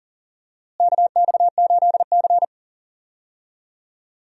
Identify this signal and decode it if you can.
Morse code, heard as a keyed tone, about 37 wpm, 700 Hz. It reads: KX8C
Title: USA Call Signs - 4 Characters - Repeated 2x - 40wpm